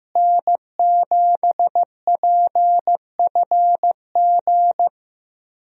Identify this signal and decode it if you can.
Morse code, heard as a keyed tone, about 15 wpm, 700 Hz. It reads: N7PFG
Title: USA Call Signs - 5 Characters - Repeated 2x - 15wpm